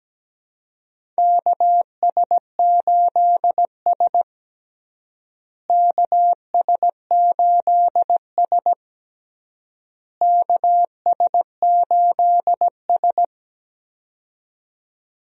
Morse code, heard as a keyed tone, about 17 wpm, 700 Hz. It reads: KS8S KS8S KS8S
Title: USA Call Signs - 4 Characters - Repeated 3x - 17wpm